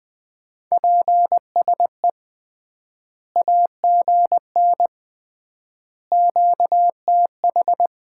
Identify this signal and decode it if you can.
Morse code, heard as a keyed tone, about 20 wpm, 700 Hz. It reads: PSE AGN QTH
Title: Common QSO Phrases - 3x Word-Spacing - 20wpm